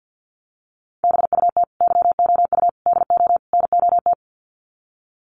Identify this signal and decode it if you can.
Morse code, heard as a keyed tone, about 50 wpm, 700 Hz. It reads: BUT YOU DO NOT
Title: Sets of 4 Words 50wpm